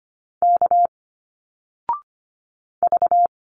Morse code, heard as a keyed tone, about 25 wpm, 700 Hz. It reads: K V
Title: Single Letters 25wpm